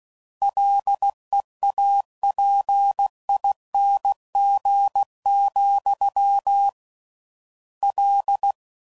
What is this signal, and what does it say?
Morse code, heard as a keyed tone, about 16 wpm, 800 Hz. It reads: LEAPING, L